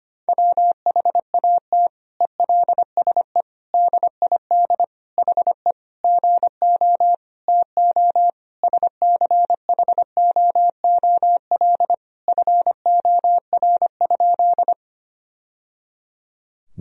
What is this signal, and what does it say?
Morse code, heard as a keyed tone, about 25 wpm, 700 Hz. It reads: WHAT ELSE DID HE GO TO SCHOOL FOR?
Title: Sentences from Top 1000 Words - Review Part 1 - 25wpm